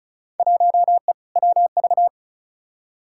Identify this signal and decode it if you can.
Morse code, heard as a keyed tone, about 34 wpm, 700 Hz. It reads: 1E WV
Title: Field Day Exchanges Encore - 35wpm